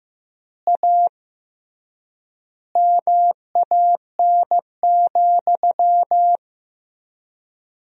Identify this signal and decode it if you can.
Morse code, heard as a keyed tone, about 15 wpm, 700 Hz. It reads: A MAN,
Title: 3x Word-Spacing - Sentences from Top 200 Words - 15wpm